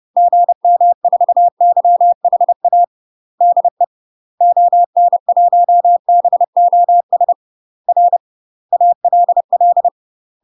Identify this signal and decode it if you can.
Morse code, heard as a keyed tone, about 30 wpm, 700 Hz. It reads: GM4YHA DE ON1BOS R ALL
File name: UK MASTER QSOs TRAINING / 30WPM / UK MASTER QSOs 0040